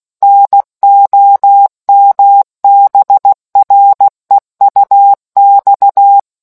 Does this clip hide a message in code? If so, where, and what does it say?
Yes: Morse code from 0.2 to 6.2 s, heard as a keyed tone, about 16 wpm, 800 Hz. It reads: NOMBREUX